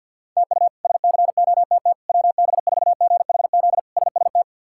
Morse code, heard as a keyed tone, about 47 wpm, 700 Hz. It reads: TU SCOTT WB4GHZ SST
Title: K1USN SST QSO Practice - 50wpm